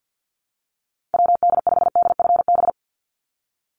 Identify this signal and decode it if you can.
Morse code, heard as a keyed tone, about 49 wpm, 700 Hz. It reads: WD5DFB
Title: USA Call Signs 50wpm